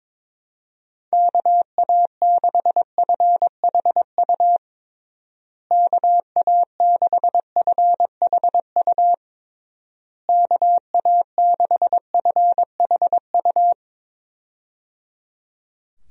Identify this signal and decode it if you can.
Morse code, heard as a keyed tone, about 22 wpm, 700 Hz. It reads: KA6FHU KA6FHU KA6FHU
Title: USA Call Signs - 6 Characters - Repeated 3x - 22wpm